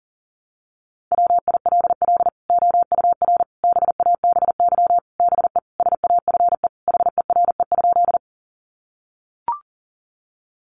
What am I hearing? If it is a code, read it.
Morse code, heard as a keyed tone, about 40 wpm, 700 Hz. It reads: WILL OUR BABY BE SAFE HERE?
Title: Sentences from Top 1300 Words 40wpm